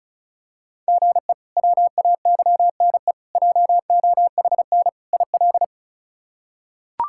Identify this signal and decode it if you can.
Morse code, heard as a keyed tone, about 34 wpm, 700 Hz. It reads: GE WAYNE JOHN IL
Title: K1USN SST QSO Practice - 35wpm